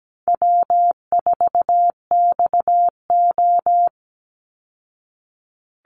Morse code, heard as a keyed tone, about 17 wpm, 700 Hz. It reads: W4XO
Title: Easy Call Signs Repeated 2x - 17wpm